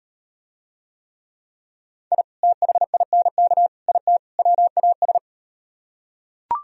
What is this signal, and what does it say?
Morse code, heard as a keyed tone, about 38 wpm, 700 Hz. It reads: I THINK IT WAS